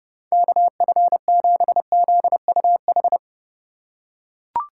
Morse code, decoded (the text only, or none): KF7ZUH